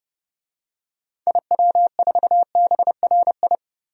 Morse code, heard as a keyed tone, about 32 wpm, 700 Hz. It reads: IW4BRI
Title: DXCC-100 Call Signs - 6 Characters - 30wpm